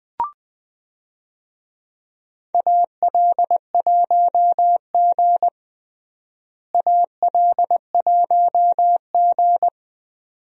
Morse code, heard as a keyed tone, about 20 wpm, 700 Hz. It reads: AL1G AL1G
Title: USA Call Signs - 4 Characters - Repeated 2x - 20wpm